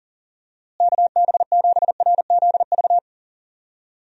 Morse code, heard as a keyed tone, about 38 wpm, 700 Hz. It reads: KB7RZV
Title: USA Call Signs - 6 Characters - Repeated 2x - 40wpm